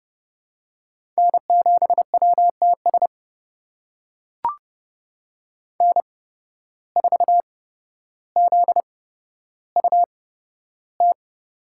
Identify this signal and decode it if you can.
Morse code, heard as a keyed tone, about 30 wpm, spaced as for 13 wpm, 700 Hz. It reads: N7WTS N4ZUT
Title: USA Call Signs - 5 Characters - 8x Character Spacing - 30wpm